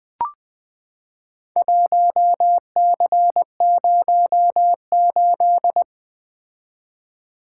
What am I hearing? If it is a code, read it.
Morse code, heard as a keyed tone, about 20 wpm, 700 Hz. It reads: 1C08
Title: Four Letter-Number Combinations - 20wpm